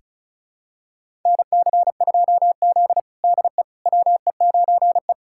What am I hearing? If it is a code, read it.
Morse code, heard as a keyed tone, about 36 wpm, 700 Hz. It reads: NC2Z DE WE9E
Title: USA Callsign DE Callsign - 4 Characters - 35wpm